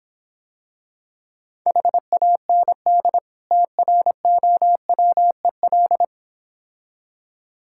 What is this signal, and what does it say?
Morse code, heard as a keyed tone, about 26 wpm, 690 Hz. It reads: HAND TROWEL